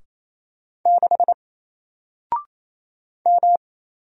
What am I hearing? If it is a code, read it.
Morse code, heard as a keyed tone, about 28 wpm, 700 Hz. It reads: 6 M
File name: Single-Letter-Number-28wpm